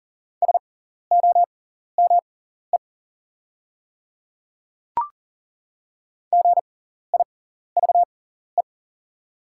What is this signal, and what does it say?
Morse code, heard as a keyed tone, about 40 wpm, spaced as for 14 wpm, 700 Hz. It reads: SOME GIVE
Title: 6x Character Spacing - Top 300 Words - 40wpm